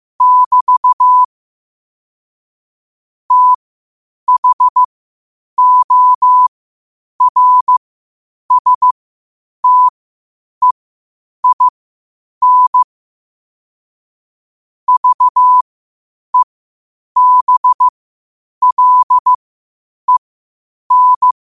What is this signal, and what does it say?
Morse code, heard as a keyed tone, about 15 wpm, spaced as for 8 wpm, 1000 Hz. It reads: <BT> THORSTEIN VEBLEN
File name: qotdfiles / 8wpm